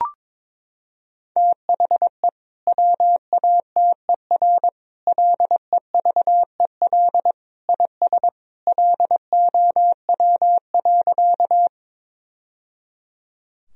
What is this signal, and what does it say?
Morse code, heard as a keyed tone, about 22 wpm, 700 Hz. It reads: THE WATER LEVEL IS LOW.
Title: Sentences from Top 1000 Words - Review Part 4 - 22wpm